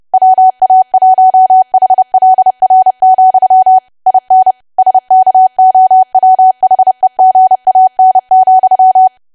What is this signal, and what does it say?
Morse code, heard as a keyed tone, about 30 wpm, 750 Hz. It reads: WA1HLR, IN SKOWHEGAN,
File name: cwa / W1AW-Level3 / 30WPM